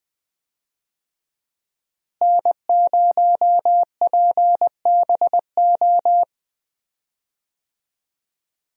Morse code, heard as a keyed tone, about 20 wpm, 700 Hz. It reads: N0PBO
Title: USA Call Signs - 5 Characters - Repeated 2x - 20wpm